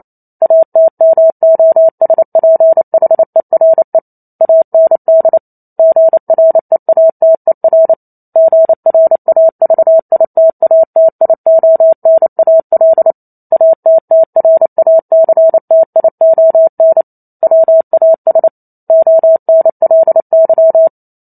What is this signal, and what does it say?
Morse code, heard as a keyed tone, about 28 wpm, 650 Hz. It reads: ATMOSPHERE AND GREATER GRAVITATIONAL ATTRACTION WAS ONLY